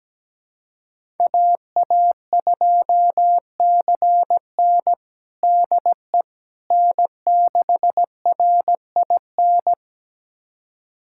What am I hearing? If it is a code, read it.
Morse code, heard as a keyed tone, about 17 wpm, 700 Hz. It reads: AA2CN DE N6RIN
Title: USA Callsign DE Callsign - 5 Characters - 17wpm